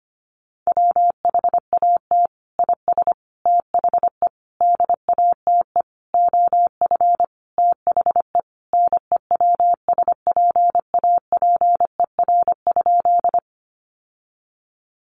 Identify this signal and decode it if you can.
Morse code, heard as a keyed tone, about 25 wpm, 700 Hz. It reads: WHAT IS THE DATE OF THE NEWSPAPER?
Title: Sentences from Top 1100 Words 25wpm